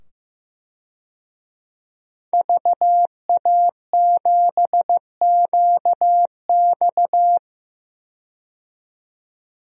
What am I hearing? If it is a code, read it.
Morse code, heard as a keyed tone, about 15 wpm, 700 Hz. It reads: VA7QX
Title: DXCC-100 Call Signs - 5 Characters - 15wpm